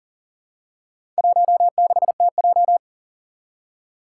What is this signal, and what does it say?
Morse code, heard as a keyed tone, about 37 wpm, 700 Hz. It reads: 16TJ
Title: Four Challenging Letter-Number Combinations 40wpm